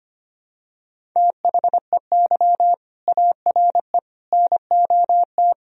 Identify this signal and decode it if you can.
Morse code, heard as a keyed tone, about 25 wpm, 700 Hz. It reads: THEY ARE NOT
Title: Sentences from Top 1000 Words - Review Part 3 - 25wpm